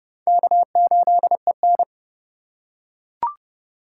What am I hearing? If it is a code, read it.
Morse code, heard as a keyed tone, about 29 wpm, 700 Hz. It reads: K8EN